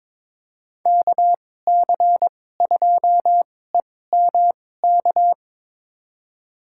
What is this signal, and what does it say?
Morse code, heard as a keyed tone, about 21 wpm, spaced as for 16 wpm, 700 Hz. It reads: KC2EMK
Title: USA Call Signs - 6 Characters - 2x Character Spacing - 22wpm